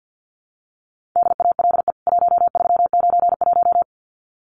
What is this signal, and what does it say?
Morse code, heard as a keyed tone, about 50 wpm, 700 Hz. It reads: DALE 1391
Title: CWT Exchange 50wpm